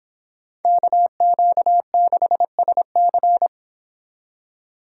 Morse code, heard as a keyed tone, about 29 wpm, 700 Hz. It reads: KQ6SC